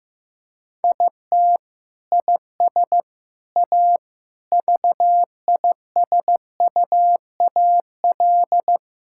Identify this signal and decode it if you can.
Morse code, heard as a keyed tone, about 15 wpm, 700 Hz. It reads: IT IS A VISUAL
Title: Sentences from Top 1000 Words - Review Part 6 - 15wpm